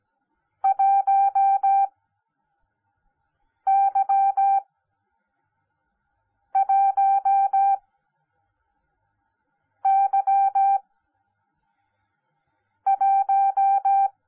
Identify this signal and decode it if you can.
Morse code, heard as a keyed tone, about 17 wpm, spaced as for 4 wpm, 790 Hz. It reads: 1Y1Y1
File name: Onlinelehrgang / morsen / DJ4UF Morsen10